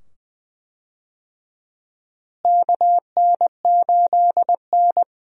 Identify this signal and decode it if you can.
Morse code, heard as a keyed tone, about 20 wpm, 700 Hz. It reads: KN8N
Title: USA Call Signs - 4 Characters - Repeated 2x - 20wpm